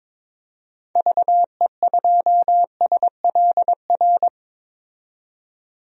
Morse code, heard as a keyed tone, about 21 wpm, 700 Hz. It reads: VE2SLR
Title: DXCC-100 Call Signs - 6 Characters - 22wpm